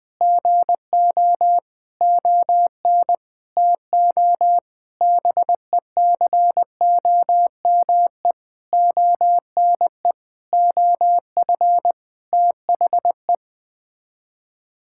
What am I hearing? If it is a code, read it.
Morse code, heard as a keyed tone, about 20 wpm, 700 Hz. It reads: GO ON TO BECOME ONE OF THE
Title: Sets of 8 Words - 20wpm